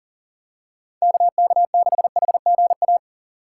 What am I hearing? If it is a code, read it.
Morse code, heard as a keyed tone, about 40 wpm, 700 Hz. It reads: KK6HGA
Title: USA Call Signs - 6 Characters - 2x Character Spacing - 40wpm